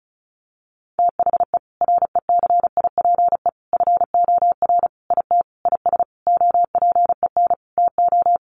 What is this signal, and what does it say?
Morse code, heard as a keyed tone, about 34 wpm, 700 Hz. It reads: THE RECIPE FOR IT IS OPEN TO
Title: Sentences from Top 1200 Words 35wpm